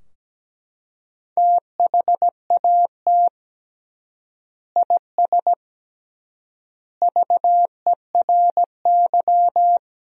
Morse code, heard as a keyed tone, about 17 wpm, 700 Hz. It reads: THAT IS VERY
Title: 3x Word-Spacing - Sentences from Top 400 Words - 17wpm